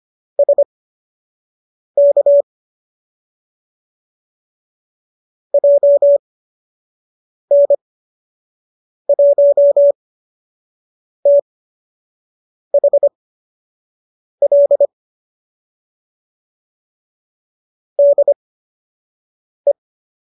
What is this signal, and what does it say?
Morse code, heard as a keyed tone, about 25 wpm, spaced as for 6 wpm, 570 Hz. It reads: SK JN1THL DE
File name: qso003 06